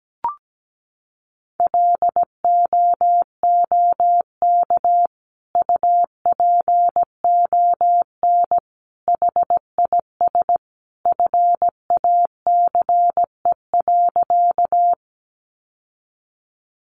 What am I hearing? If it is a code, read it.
Morse code, heard as a keyed tone, about 17 wpm, 700 Hz. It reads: LOOK UPON HIS FACE.
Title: Sentences from Top 1100 Words 17wpm